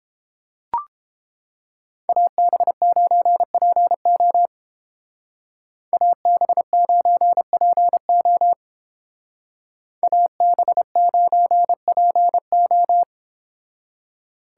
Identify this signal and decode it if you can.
Morse code, heard as a keyed tone, about 29 wpm, 700 Hz. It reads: AB9PO AB9PO AB9PO